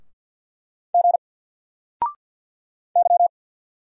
Morse code, heard as a keyed tone, about 48 wpm, 700 Hz. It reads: G Y
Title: Single Letter-Number 50wpm